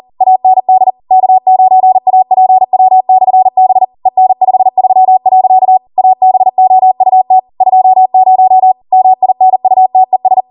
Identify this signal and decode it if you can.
Morse code, heard as a keyed tone, about 40 wpm, 750 Hz. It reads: AND K9APW/B EN53. ABOUT 20 MINUTES